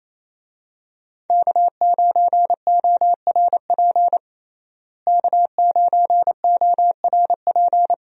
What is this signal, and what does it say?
Morse code, heard as a keyed tone, about 28 wpm, 700 Hz. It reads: K9ORP K9ORP
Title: USA Call Signs - 5 Characters - Repeated 2x - 28wpm